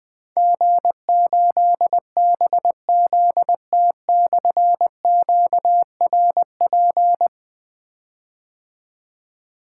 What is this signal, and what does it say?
Morse code, heard as a keyed tone, about 20 wpm, 700 Hz. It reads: G8BZT/QRP
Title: Difficult Call Signs 20wpm